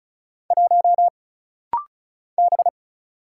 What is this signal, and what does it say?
Morse code, heard as a keyed tone, about 36 wpm, 700 Hz. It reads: 1 B
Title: Single Letter-Number - Rapid-Fire 35wpm